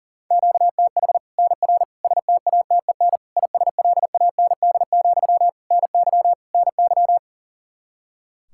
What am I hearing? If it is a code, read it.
Morse code, heard as a keyed tone, about 40 wpm, 700 Hz. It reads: QTH NR STATEN ISLAND, NY NY
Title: Practice QSO - Exchange 1 - Beginning - 40wpm